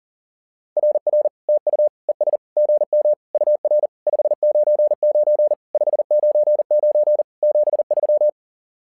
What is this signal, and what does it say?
Morse code, heard as a keyed tone, about 40 wpm, 600 Hz. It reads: RR TU ES GM UR 599 599 73
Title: SOTA QSO Practice 40wpm